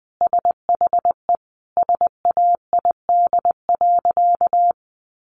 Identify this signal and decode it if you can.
Morse code, heard as a keyed tone, about 20 wpm, 700 Hz. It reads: SHE SAID.
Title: Sentences from Top 1100 Words 20wpm